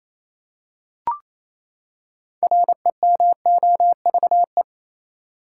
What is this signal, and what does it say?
Morse code, heard as a keyed tone, about 27 wpm, 700 Hz. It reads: REMOVE
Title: Six Letter Words - 28wpm